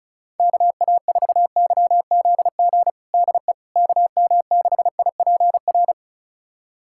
Morse code, heard as a keyed tone, about 36 wpm, 700 Hz. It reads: KA4YZG DE KM6IPR
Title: USA Callsign DE Callsign - 6 Characters - 35wpm